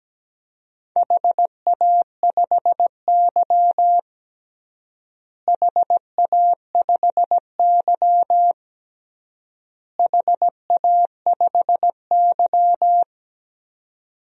Morse code, heard as a keyed tone, about 17 wpm, 700 Hz. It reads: HA5Y HA5Y HA5Y
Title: DXCC-100 Call Signs - 4 Characters - Repeated 3x - 17wpm